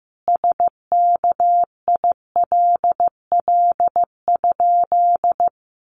Morse code, heard as a keyed tone, about 15 wpm, 700 Hz. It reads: SKILL?
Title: Sentences from Top 1200 Words 15wpm